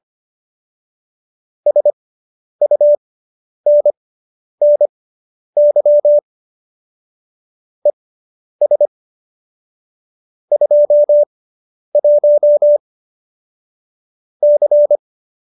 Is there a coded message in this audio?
Yes: Morse code from 1.7 to 15.0 s, heard as a keyed tone, about 25 wpm, spaced as for 10 wpm, 590 Hz. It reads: SUNNY ES 21 C